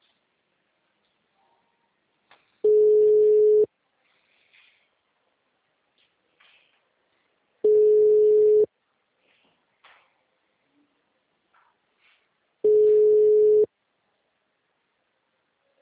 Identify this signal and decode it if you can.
Ringback tone (Europe, 425 Hz).
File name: out- -2003